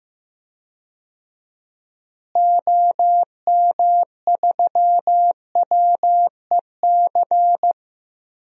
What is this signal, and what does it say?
Morse code, heard as a keyed tone, about 15 wpm, 700 Hz. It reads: OM3WEC
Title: DXCC-100 Call Signs - 6 Characters - Repeated 2x - 15wpm